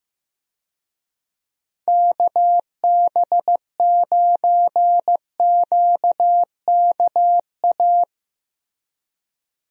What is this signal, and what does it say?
Morse code, heard as a keyed tone, about 16 wpm, 700 Hz. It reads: KB9QKA
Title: USA Call Signs - 6 Characters - Repeated 2x - 15wpm